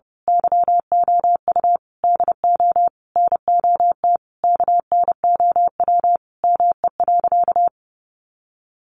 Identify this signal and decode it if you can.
Morse code, heard as a keyed tone, about 30 wpm, 700 Hz. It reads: YOU DO NOT KNOW ME.
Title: Sentences from Top 600 Words 30wpm